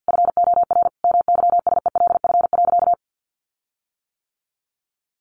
Morse code, heard as a keyed tone, about 50 wpm, 700 Hz. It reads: FOR MYSELF.
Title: Sentences from Top 1300 Words 50wpm